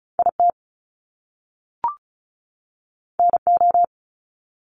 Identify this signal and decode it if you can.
Morse code, heard as a keyed tone, about 36 wpm, 700 Hz. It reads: IT NO
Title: Top Two Letter Combinations in Words 35wpm